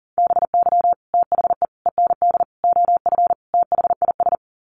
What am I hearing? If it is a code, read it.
Morse code, heard as a keyed tone, about 40 wpm, 700 Hz. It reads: BY THE END OF THIS
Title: Sets of 5 Words Encore 40wpm